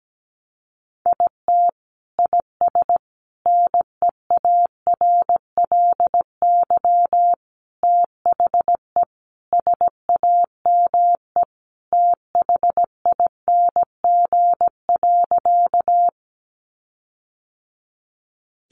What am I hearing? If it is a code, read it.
Morse code, heard as a keyed tone, about 17 wpm, 700 Hz. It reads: IT IS NEARLY THE SAME THING.
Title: Sentences from Top 1300 Words 17wpm